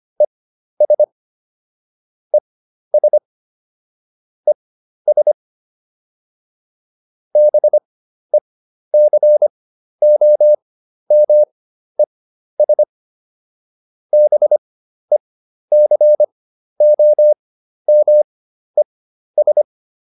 Morse code, heard as a keyed tone, about 25 wpm, spaced as for 12 wpm, 600 Hz. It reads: ES ES ES BECOMES BECOMES